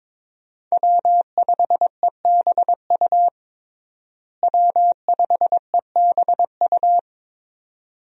Morse code, heard as a keyed tone, about 22 wpm, 700 Hz. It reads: W5EBU W5EBU
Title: USA Call Signs - 5 Characters - Repeated 2x - 22wpm